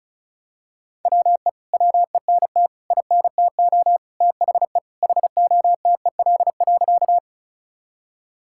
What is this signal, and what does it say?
Morse code, heard as a keyed tone, about 36 wpm, 700 Hz. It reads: WE WENT INTO THE HOTEL.
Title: Sentences from Top 1600 Words 35wpm